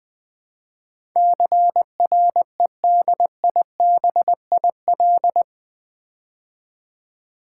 Morse code, heard as a keyed tone, about 20 wpm, 700 Hz. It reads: CREDIBIL
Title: Build A Word - 4 Letters At A Time - 11-Letter Words - 20wpm